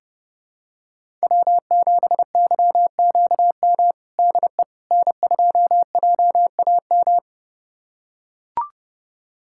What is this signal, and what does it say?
Morse code, heard as a keyed tone, about 30 wpm, 700 Hz. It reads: W7YQM DE N2JAM
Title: USA Callsign DE Callsign - 5 Characters - 30wpm